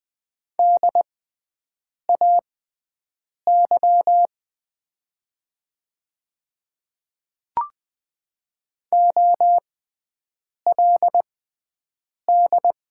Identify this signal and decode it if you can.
Morse code, heard as a keyed tone, about 20 wpm, spaced as for 7 wpm, 700 Hz. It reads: DAY OLD
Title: 6x Character Spacing - Top 200 Words - 20wpm